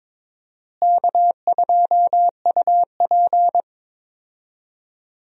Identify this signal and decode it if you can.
Morse code, heard as a keyed tone, about 22 wpm, 700 Hz. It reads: K2UP
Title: USA Call Signs - 4 Characters - Repeated 3x - 22wpm